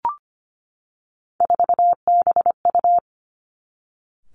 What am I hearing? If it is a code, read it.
Morse code, heard as a keyed tone, about 25 wpm, 700 Hz. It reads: 4BU
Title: Three Challenging Letter-Number Combinations 25wpm